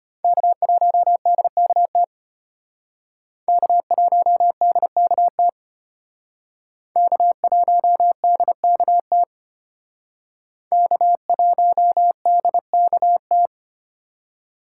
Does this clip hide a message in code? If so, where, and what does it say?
Yes: Morse code from 0.2 to 13.5 s, heard as a keyed tone, about 32 wpm, 700 Hz. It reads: K1DKT K1DKT K1DKT K1DKT